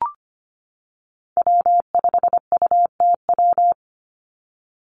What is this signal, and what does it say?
Morse code, heard as a keyed tone, about 25 wpm, 700 Hz. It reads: W5UTW
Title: Call Signs - 25wpm